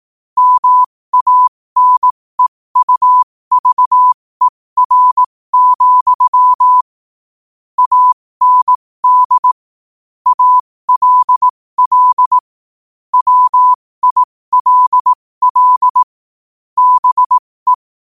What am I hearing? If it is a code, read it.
Morse code, heard as a keyed tone, about 18 wpm, spaced as for 15 wpm, 1000 Hz. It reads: MANEUVER, AND ALL WILL BE
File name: qotdfiles / 15wpm